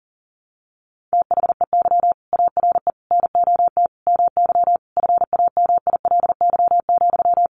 Morse code, heard as a keyed tone, about 40 wpm, 700 Hz. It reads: THEY ARE NOT MY FAMILY,
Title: Sentences from Top 1100 Words 40wpm